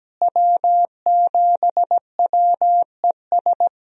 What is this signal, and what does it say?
Morse code, heard as a keyed tone, about 17 wpm, 700 Hz. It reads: W7WES
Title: USA Call Signs - 5 Characters - Repeated 2x - 17wpm